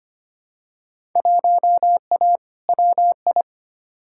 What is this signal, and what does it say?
Morse code, heard as a keyed tone, about 25 wpm, 700 Hz. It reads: 1A WI
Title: Field Day Exchanges - 25wpm